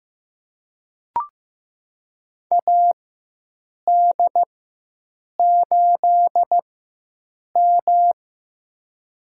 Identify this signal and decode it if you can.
Morse code, heard as a keyed tone, about 15 wpm, spaced as for 7 wpm, 700 Hz. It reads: AD8M
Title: USA Call Signs - 4x Character Spacing - 15wpm